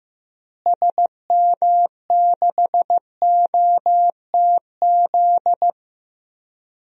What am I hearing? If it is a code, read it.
Morse code, heard as a keyed tone, about 15 wpm, 700 Hz. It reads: SM6OTZ
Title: DXCC-100 Call Signs - 6 Characters - Repeated 2x - 15wpm